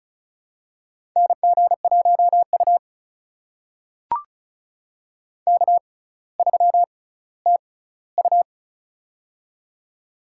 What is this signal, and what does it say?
Morse code, heard as a keyed tone, about 36 wpm, spaced as for 18 wpm, 700 Hz. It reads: NG1U K3TU
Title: USA Call Signs - 4 Characters - 6x Character Spacing - 35wpm